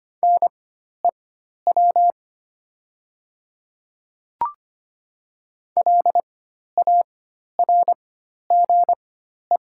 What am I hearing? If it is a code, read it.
Morse code, heard as a keyed tone, about 25 wpm, spaced as for 12 wpm, 690 Hz. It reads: NEW LARGE
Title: 4x Character Spacing - Top 200 Words - 25wpm